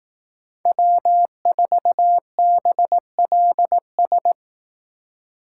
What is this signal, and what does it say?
Morse code, heard as a keyed tone, about 18 wpm, 700 Hz. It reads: W4BLS
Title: Speed Racing - USA 1x3 Call Signs 15wpm